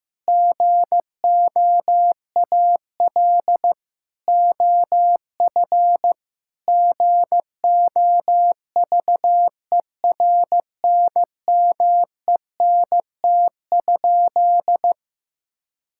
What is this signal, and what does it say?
Morse code, heard as a keyed tone, about 16 wpm, 700 Hz. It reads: GOAL OF GOVERNMENT?
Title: Sentences from Top 1000 Words - Review Part 5 - 15wpm